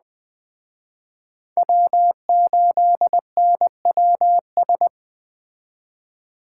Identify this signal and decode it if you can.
Morse code, heard as a keyed tone, about 20 wpm, 700 Hz. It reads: W8NWS
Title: Speed Racing - USA 1x3 Call Signs 17wpm